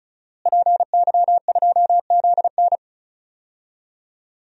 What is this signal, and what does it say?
Morse code, heard as a keyed tone, about 34 wpm, 700 Hz. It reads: PY2ZN
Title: DXCC-100 Call Signs - 5 Characters - Repeated 3x - 35wpm